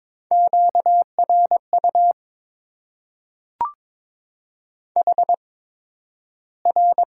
Morse code, heard as a keyed tone, about 22 wpm, spaced as for 12 wpm, 700 Hz. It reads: QRU HR
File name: Top 90 QSO Elements - 8x Character Spacing - 22wpm